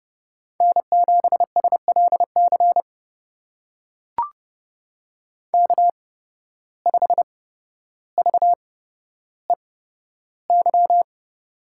Morse code, heard as a keyed tone, about 30 wpm, spaced as for 13 wpm, 700 Hz. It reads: N7SLC K5VEY
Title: USA Call Signs - 5 Characters - 8x Character Spacing - 30wpm